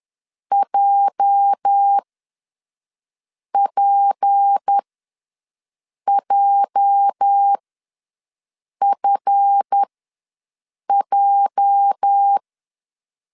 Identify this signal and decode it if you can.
Morse code, heard as a keyed tone, about 10 wpm, spaced as for 5 wpm, 800 Hz. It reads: JPJFJ
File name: LEZIONI / 8 YJ45 / 8 YJ ws 10wpm 4043